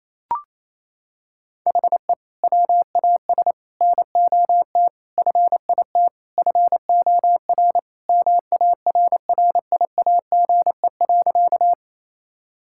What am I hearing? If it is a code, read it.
Morse code, heard as a keyed tone, about 27 wpm, 700 Hz. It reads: HE WAS NOT FIT FOR MARRIAGE.
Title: Sentences from Top 1000 Words - Review Part 5 - 28wpm